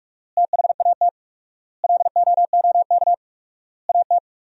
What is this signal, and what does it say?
Morse code, heard as a keyed tone, about 46 wpm, 700 Hz. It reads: THAT LOOK AT
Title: 4x Word-Spacing - Sentences from Top 200 Words - 45wpm